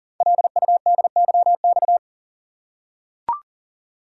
Morse code, heard as a keyed tone, about 41 wpm, 690 Hz. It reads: LUDYX
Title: Code Groups - 5 Letters - 40wpm